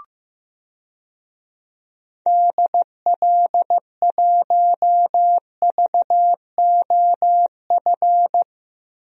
Morse code, heard as a keyed tone, about 16 wpm, 700 Hz. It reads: DL1VOF
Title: DXCC-100 Call Signs - 6 Characters - 15wpm